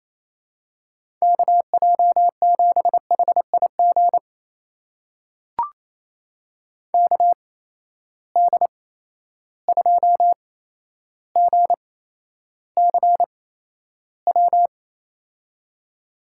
Morse code, heard as a keyed tone, about 29 wpm, spaced as for 12 wpm, 700 Hz. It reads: KJ7HIG KD2GCW
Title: USA Call Signs - 6 Characters - 8x Character Spacing - 28wpm